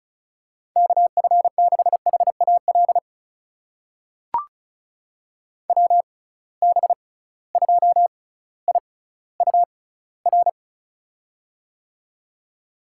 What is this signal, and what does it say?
Morse code, heard as a keyed tone, about 36 wpm, spaced as for 18 wpm, 690 Hz. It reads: KF6HAL WB2IUR
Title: USA Call Signs - 6 Characters - 6x Character Spacing - 35wpm